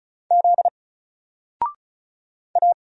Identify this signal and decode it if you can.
Morse code, heard as a keyed tone, about 35 wpm, 700 Hz. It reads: Z A
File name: Single-Letter-Number-35wpm